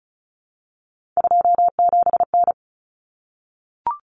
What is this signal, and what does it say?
Morse code, heard as a keyed tone, about 34 wpm, 700 Hz. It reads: 27N
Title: Common Cut Numbers - 3-Digit 35wpm